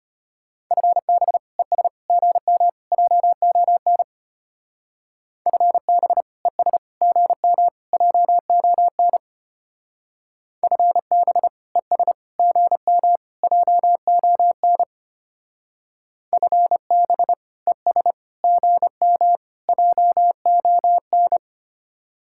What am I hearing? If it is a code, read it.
Morse code, heard as a keyed tone, about 31 wpm, 700 Hz. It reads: FB ES GM JON FB ES GM JON FB ES GM JON FB ES GM JON
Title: Speed-Racing Common QSO Phrases - 25wpm